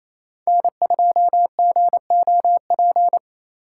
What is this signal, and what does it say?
Morse code, heard as a keyed tone, about 35 wpm, 700 Hz. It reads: N2GOP
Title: Speed Racing - USA Call Signs 28wpm